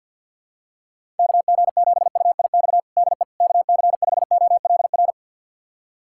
Morse code, heard as a keyed tone, about 52 wpm, 700 Hz. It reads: KG7UIX DE KC5OLR
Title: USA Callsign DE Callsign - 6 Characters - 50wpm